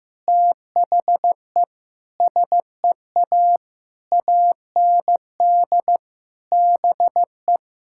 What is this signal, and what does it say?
Morse code, heard as a keyed tone, about 16 wpm, 700 Hz. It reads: THE SEA AND BE
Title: Sentences from Top 1000 Words - Review Part 5 - 15wpm